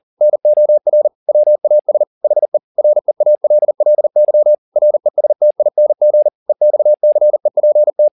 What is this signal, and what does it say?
Morse code, heard as a keyed tone, about 40 wpm, 600 Hz. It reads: NOR WAS HE REALLY RESTING EXCEPT
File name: the old man and the sea 2 40wpm